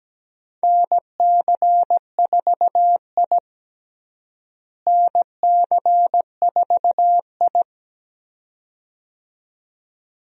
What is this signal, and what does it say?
Morse code, heard as a keyed tone, about 17 wpm, 700 Hz. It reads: NC4I NC4I
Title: USA Call Signs - 4 Characters - Repeated 2x - 17wpm